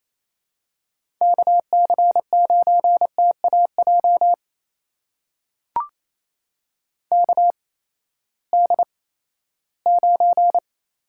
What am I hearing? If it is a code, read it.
Morse code, heard as a keyed tone, about 28 wpm, spaced as for 16 wpm, 700 Hz. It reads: KC9TAJ KD9
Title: USA Call Signs - 6 Characters - 8x Character Spacing - 28wpm